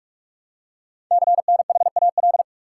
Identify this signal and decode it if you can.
Morse code, heard as a keyed tone, about 44 wpm, 700 Hz. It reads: CNHAL
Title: Code Groups - 5 Letters - 45wpm